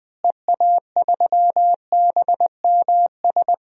Morse code, heard as a keyed tone, about 20 wpm, 700 Hz. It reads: EA3BMS
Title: DXCC-100 Call Signs - 6 Characters - Repeated 2x - 20wpm